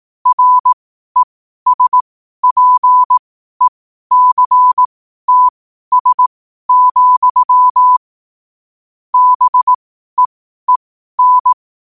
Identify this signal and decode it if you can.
Morse code, heard as a keyed tone, about 18 wpm, spaced as for 12 wpm, 1000 Hz. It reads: RESPECTS, BEEN
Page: Quotes for Wed, 13 Aug 2025 in Morse Code at 12 words per minute.